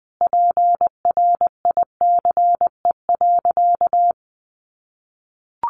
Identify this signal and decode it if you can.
Morse code, heard as a keyed tone, about 20 wpm, 700 Hz. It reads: PRICE.
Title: Sentences from Top 600 Words 20wpm